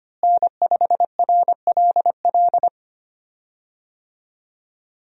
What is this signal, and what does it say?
Morse code, heard as a keyed tone, about 25 wpm, 700 Hz. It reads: N5RLL
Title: USA Call Signs - 5 Characters - Repeated 2x - 25wpm